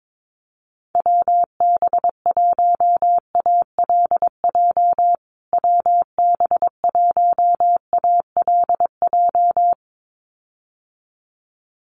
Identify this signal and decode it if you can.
Morse code, heard as a keyed tone, about 22 wpm, 700 Hz. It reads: WB1ALJ WB1ALJ
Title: USA Call Signs Repeated 2x 22wpm